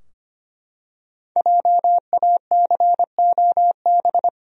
Morse code, heard as a keyed tone, about 25 wpm, 700 Hz. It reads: JACOB
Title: USA Common Names - Max 5 Letters - 25wpm